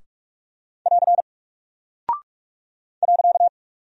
Morse code, heard as a keyed tone, about 44 wpm, 700 Hz. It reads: <AR> .
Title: Single Character (Letter:Number:Punctuation:Prosign) Alt 45wpm